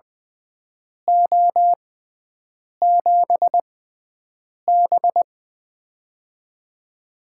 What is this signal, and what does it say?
Morse code, heard as a keyed tone, about 20 wpm, spaced as for 7 wpm, 700 Hz. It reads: O7B
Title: USA Call Signs - 4 Characters - 6x Character Spacing - 20wpm